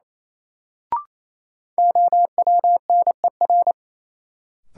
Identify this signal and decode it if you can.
Morse code, heard as a keyed tone, about 28 wpm, 700 Hz. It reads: OWNER
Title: Sentences from Top 1600 Words - New Word Drill - 28wpm